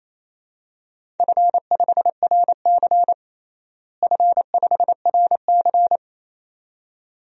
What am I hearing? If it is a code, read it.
Morse code, heard as a keyed tone, about 28 wpm, 700 Hz. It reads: F5RC F5RC
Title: DXCC-100 Call Signs - 4 Characters - Repeated 2x - 28wpm